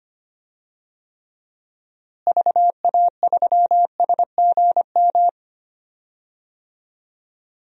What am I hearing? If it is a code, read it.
Morse code, heard as a keyed tone, about 25 wpm, 700 Hz. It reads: VA3SGM
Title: DXCC-100 Call Signs - 6 Characters - Repeated 2x - 25wpm